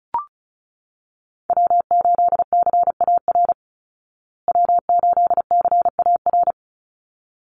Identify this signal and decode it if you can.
Morse code, heard as a keyed tone, about 34 wpm, 700 Hz. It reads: W8CAR W8CAR
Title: Call Signs Repeated 3x - 35wpm